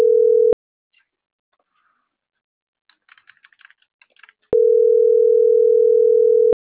Ringback tone (North America), 440 + 480 Hz.
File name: CENTRAL MABE / out maria - .61250